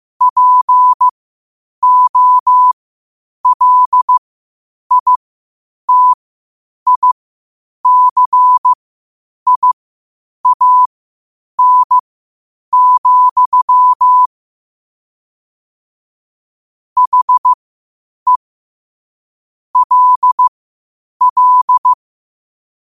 Morse code, heard as a keyed tone, about 15 wpm, spaced as for 8 wpm, 1000 Hz. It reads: POLITICIAN, HE LL
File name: qotdfiles / 8wpm